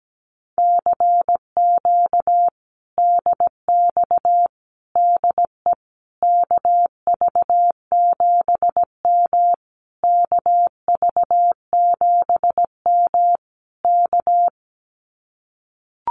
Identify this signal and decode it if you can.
Morse code, heard as a keyed tone, about 17 wpm, 700 Hz. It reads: CQ DX DE KV7M KV7M K
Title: Common QSO Phrases Encore 17wpm